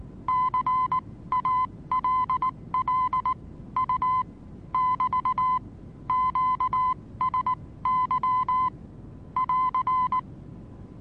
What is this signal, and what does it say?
Morse code, heard as a keyed tone, about 19 wpm, spaced as for 16 wpm, 1020 Hz. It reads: CALL U <BT> QSY <AR>
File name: repos / Morsekurs / DJ4UF Morsen26